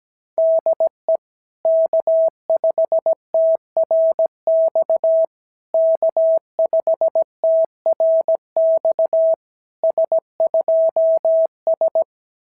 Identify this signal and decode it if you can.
Morse code, heard as a keyed tone, about 17 wpm, 650 Hz. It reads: DE K5TRX K5TRX S2S
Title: SOTA QSO Practice 17wpm